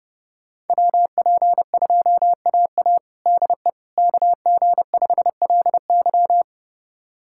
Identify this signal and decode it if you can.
Morse code, heard as a keyed tone, about 30 wpm, 700 Hz. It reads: WP2AA DE KG5LY
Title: USA Callsign DE Callsign - 5 Characters - 30wpm